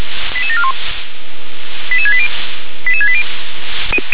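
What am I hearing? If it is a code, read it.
Five-tone selective call (ZVEI), heard as quick runs of five tones, tones of about 70 ms. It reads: 88741 88680 88680
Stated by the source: SDR-received ZVEI five-tone call, code 88856.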